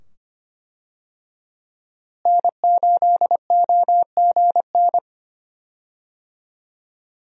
Morse code, heard as a keyed tone, about 25 wpm, 700 Hz. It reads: N8OGN
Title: USA Call Signs - 5 Characters - Repeated 2x - 25wpm